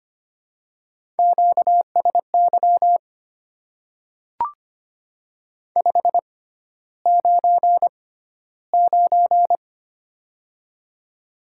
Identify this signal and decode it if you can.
Morse code, heard as a keyed tone, about 25 wpm, spaced as for 13 wpm, 700 Hz. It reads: QSY 599
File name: Top 90 QSO Elements - 6x Character Spacing - 25wpm